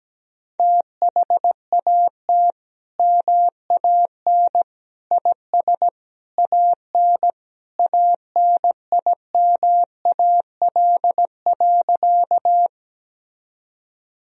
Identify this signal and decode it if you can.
Morse code, heard as a keyed tone, about 17 wpm, 700 Hz. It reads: THAT MAN IS AN ANIMAL.
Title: Sentences from Top 1000 Words - Review Part 1 - 17wpm